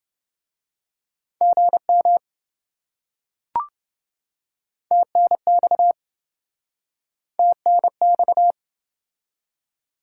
Morse code, heard as a keyed tone, about 26 wpm, 700 Hz. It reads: GM TNX TNX